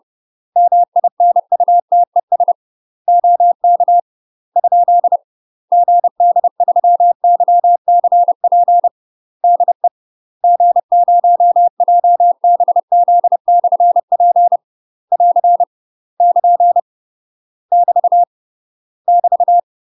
Morse code, heard as a keyed tone, about 30 wpm, 700 Hz. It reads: MINUTES OK ? GD3YCP DE G0JBZ/P <AR> <KN> <BT> <BT>
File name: UK MASTER QSOs TRAINING / 30WPM / UK MASTER QSOs 0026